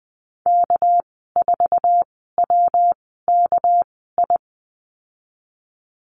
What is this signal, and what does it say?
Morse code, heard as a keyed tone, about 19 wpm, spaced as for 14 wpm, 700 Hz. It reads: K4WKI
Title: 200 Lesson 32g - Letter k - Call Signs 20wpm